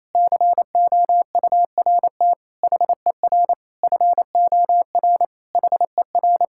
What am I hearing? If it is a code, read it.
Morse code, heard as a keyed tone, about 28 wpm, 700 Hz. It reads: COURT HER FOR HER
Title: Sentences from Top 1600 Words 28wpm